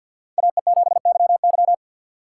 Morse code, heard as a keyed tone, about 49 wpm, 700 Hz. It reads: AE7YY
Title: USA Call Signs - 5 Characters - 6x Character Spacing - 50wpm